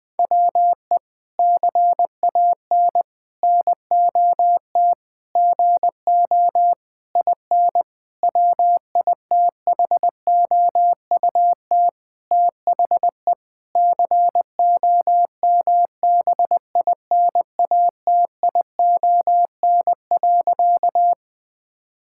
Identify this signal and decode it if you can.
Morse code, heard as a keyed tone, about 21 wpm, 700 Hz. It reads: WE CAN NOT GO IN WITHOUT THE COMBINATION.
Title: Sentences from Top 1000 Words - Review Part 6 - 20wpm